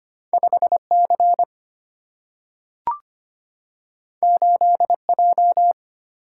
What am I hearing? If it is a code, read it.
Morse code, heard as a keyed tone, about 25 wpm, 700 Hz. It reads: 5 C 8 J
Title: Single-Letter Single-Number 25wpm